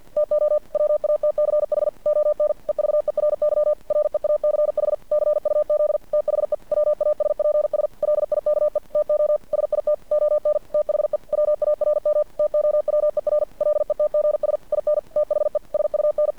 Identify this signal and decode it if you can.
Morse code, heard as a keyed tone, about 50 wpm, 600 Hz. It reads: TO WATCH ON EVERY REACH CUZ THE WASPS LIKE TO SIT ON THE WARM TOWER LETGS IN THE SUN
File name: archived / CRSnet